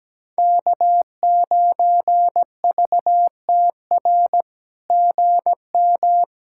Morse code, heard as a keyed tone, about 17 wpm, 700 Hz. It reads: K9VTR GM
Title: SOTA QSO Practice 17wpm